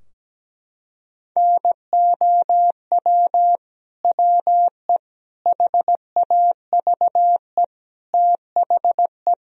Morse code, heard as a keyed tone, about 17 wpm, 700 Hz. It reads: NOW WE HAVE THE
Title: Sentences from Top 1000 Words - Review Part 3 - 17wpm